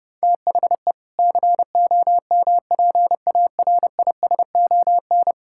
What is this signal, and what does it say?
Morse code, heard as a keyed tone, about 30 wpm, 700 Hz. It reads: THE COMPARISON
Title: Sentences from Top 1600 Words 30wpm